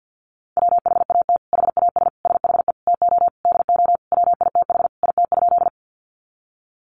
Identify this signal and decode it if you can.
Morse code, heard as a keyed tone, about 50 wpm, 700 Hz. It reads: WHAT HAS SHE TO DO WITH IT?
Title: Sentences from Top 100 Words 50wpm